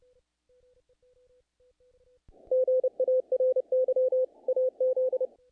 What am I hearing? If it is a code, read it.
Morse code, heard as a keyed tone, about 28 wpm, 510 Hz. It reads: GARY AZ
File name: contests / naqp-cw-k5tr-op-k5pi- -1800